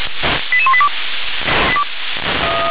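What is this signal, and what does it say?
Five-tone selective call (ZVEI), heard as quick runs of five tones, tones of about 70 ms. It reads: 88182
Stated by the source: ZVEI five-tone selective call received by SDR.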